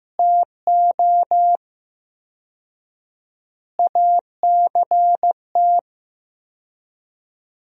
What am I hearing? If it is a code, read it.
Morse code, heard as a keyed tone, about 15 wpm, 700 Hz. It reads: TO ACT
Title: 4x Word-Spacing - Sentences from Top 500 Words - 15wpm